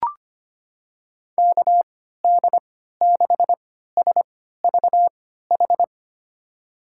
Morse code, heard as a keyed tone, about 25 wpm, spaced as for 14 wpm, 700 Hz. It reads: KD6SVH
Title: USA Call Signs - 6 Characters - 3x Character Spacing - 25wpm